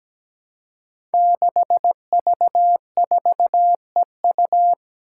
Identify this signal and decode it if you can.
Morse code, heard as a keyed tone, about 17 wpm, 700 Hz. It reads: 6V4EU
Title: Five Challenging Letter-Number Combinations 17wpm